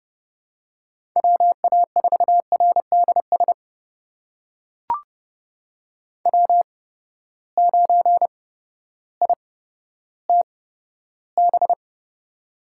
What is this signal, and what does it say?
Morse code, heard as a keyed tone, about 30 wpm, spaced as for 14 wpm, 700 Hz. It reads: WA4RDS W9ITB
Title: USA Call Signs - 8x Character Spacing - 30wpm